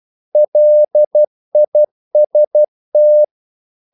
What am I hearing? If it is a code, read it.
Morse code, heard as a keyed tone, about 12 wpm, 600 Hz. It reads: LIST
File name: CW / Practice / EnglishQuotations0312